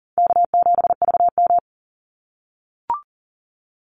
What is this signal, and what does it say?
Morse code, heard as a keyed tone, about 40 wpm, 700 Hz. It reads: K7VM
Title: Call Signs - 40wpm